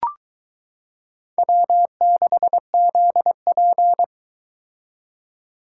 Morse code, heard as a keyed tone, about 22 wpm, 700 Hz. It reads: W6ZP